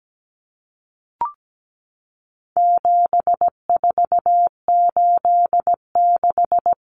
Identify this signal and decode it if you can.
Morse code, heard as a keyed tone, about 17 wpm, 700 Hz. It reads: 7486
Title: Common Cut Numbers - 4-Digit 17wpm